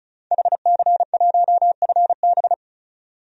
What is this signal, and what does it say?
Morse code, heard as a keyed tone, about 36 wpm, 700 Hz. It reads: HC1FB
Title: DXCC-100 Call Signs - 5 Characters - 35wpm